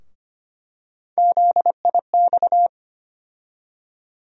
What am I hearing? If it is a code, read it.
Morse code, heard as a keyed tone, about 25 wpm, 700 Hz. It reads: ZIX
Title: USA Call Sign Postfixes - 3 Letters 25wpm